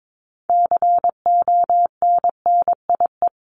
Morse code, heard as a keyed tone, about 22 wpm, 700 Hz. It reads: CONNIE
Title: CWT Exchange 22wpm